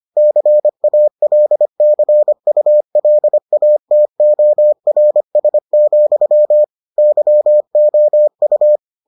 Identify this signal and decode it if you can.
Morse code, heard as a keyed tone, about 25 wpm, 600 Hz. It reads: CALCULATORS, YOU